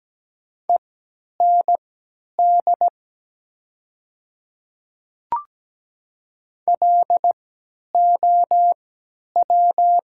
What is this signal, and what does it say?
Morse code, heard as a keyed tone, about 17 wpm, spaced as for 10 wpm, 700 Hz. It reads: END LOW
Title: 3x Character Spacing - Top 200 Words - 17wpm